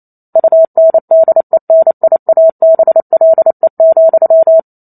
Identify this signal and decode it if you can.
Morse code, heard as a keyed tone, about 28 wpm, 650 Hz. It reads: UNDENIABLE,